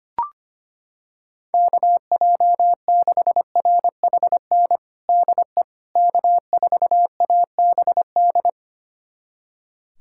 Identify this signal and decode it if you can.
Morse code, heard as a keyed tone, about 25 wpm, 700 Hz. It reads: KJ6RHN DE K4ABD
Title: USA Callsign DE Callsign 25wpm